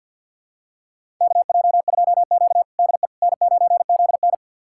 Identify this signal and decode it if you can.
Morse code, heard as a keyed tone, about 53 wpm, 700 Hz. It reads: KJ2Q DE N9ZN
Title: USA Callsign DE Callsign - 4 Characters - 50wpm